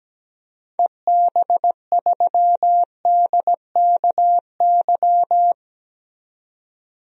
Morse code, heard as a keyed tone, about 17 wpm, 700 Hz. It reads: EB3DKY
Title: DXCC-100 Call Signs - 6 Characters - 17wpm